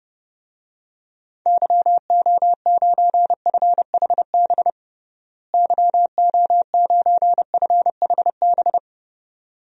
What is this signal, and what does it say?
Morse code, heard as a keyed tone, about 31 wpm, 700 Hz. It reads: YO9FHB YO9FHB
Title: DXCC-100 Call Signs - 6 Characters - Repeated 2x - 30wpm